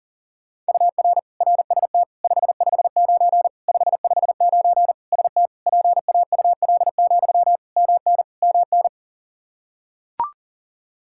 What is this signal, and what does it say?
Morse code, heard as a keyed tone, about 40 wpm, 700 Hz. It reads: UR RST 559 559 ST PAUL, MN MN
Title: Practice QSO - Exchange 1 - Beginning - 40wpm